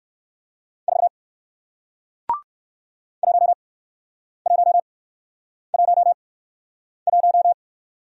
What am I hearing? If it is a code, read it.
Morse code, heard as a keyed tone, about 58 wpm, spaced as for 9 wpm, 700 Hz. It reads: 4 1111